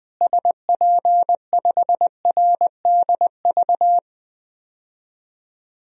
Morse code, heard as a keyed tone, about 20 wpm, 700 Hz. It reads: SP5RDV
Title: DXCC-100 Call Signs - 6 Characters - Repeated 3x - 20wpm